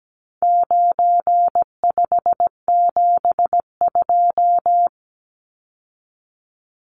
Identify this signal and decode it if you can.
Morse code, heard as a keyed tone, about 17 wpm, 700 Hz. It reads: 9572
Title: Numbers - 4-Digit 17wpm